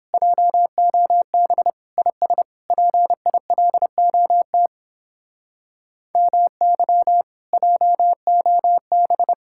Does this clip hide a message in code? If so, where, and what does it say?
Yes: Morse code from 0.1 to 9.3 s, heard as a keyed tone, about 31 wpm, 700 Hz. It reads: JOB IS PILOT MY JOB